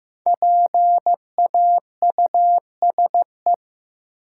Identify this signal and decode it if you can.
Morse code, heard as a keyed tone, about 15 wpm, 700 Hz. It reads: PAUSE
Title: Five Letter Words - 15wpm